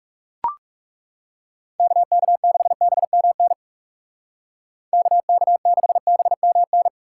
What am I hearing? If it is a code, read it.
Morse code, heard as a keyed tone, about 36 wpm, 700 Hz. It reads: KK6BMN KK6BMN
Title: Speed Racing - USA 2x3 Call Signs 30wpm